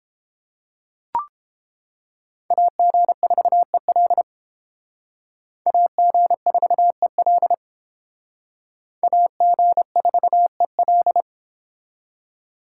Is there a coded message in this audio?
Yes: Morse code from 2.5 to 11.2 s, heard as a keyed tone, about 31 wpm, 700 Hz. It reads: AG4EL AG4EL AG4EL